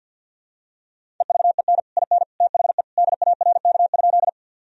Morse code, heard as a keyed tone, about 52 wpm, 700 Hz. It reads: EVEN IN THE DARK?
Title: Sentences from Top 1000 Words - Review Part 1 - 50wpm